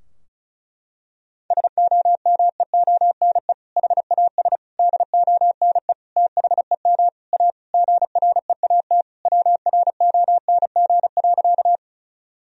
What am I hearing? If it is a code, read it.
Morse code, heard as a keyed tone, about 36 wpm, 700 Hz. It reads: SOMEONE HAS DONE THEM A GREAT WRONG.
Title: Sentences from Top 1000 Words - Review Part 4 - 35wpm